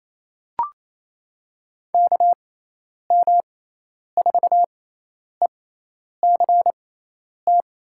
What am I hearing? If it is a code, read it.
Morse code, heard as a keyed tone, about 28 wpm, spaced as for 10 wpm, 700 Hz. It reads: KM4ECT
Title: USA Call Signs - 6x Character Spacing - 28wpm